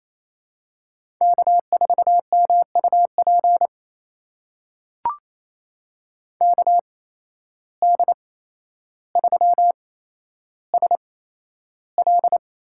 Morse code, heard as a keyed tone, about 28 wpm, spaced as for 12 wpm, 700 Hz. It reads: K4MUP KD3SL
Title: USA Call Signs - 5 Characters - 8x Character Spacing - 28wpm